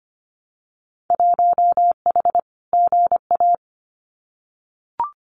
Morse code, heard as a keyed tone, about 25 wpm, 700 Hz. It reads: 1H GA
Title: Winter Field Day Exchanges 25wpm